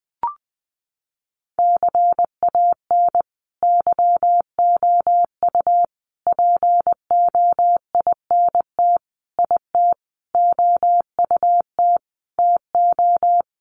Morse code, heard as a keyed tone, about 20 wpm, 700 Hz. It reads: CAN YOU POINT IT OUT TO
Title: Sentences from Top 200 Words 20wpm